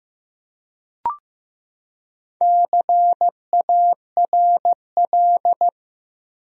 Morse code, heard as a keyed tone, about 15 wpm, 700 Hz. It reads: CARL
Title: USA Common Names - Max 5 Letters - 15wpm